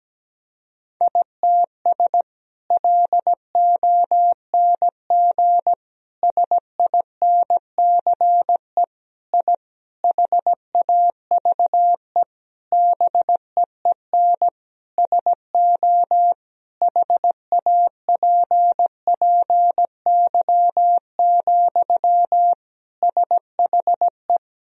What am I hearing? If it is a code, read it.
Morse code, heard as a keyed tone, about 17 wpm, 700 Hz. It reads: ITS LONG SINCE I HAVE BEEN SO HAPPY, SHE
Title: Sentences from Top 1000 Words - Review Part 1 - 17wpm